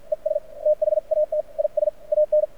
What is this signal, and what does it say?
Morse code, heard as a keyed tone, about 53 wpm, spaced as for 45 wpm, 600 Hz. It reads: ES THAT IS AN
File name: archived / CRSnet